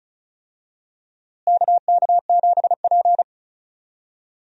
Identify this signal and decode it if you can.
Morse code, heard as a keyed tone, about 36 wpm, 700 Hz. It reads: KK7P
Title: USA Call Signs - 4 Characters - Repeated 2x - 35wpm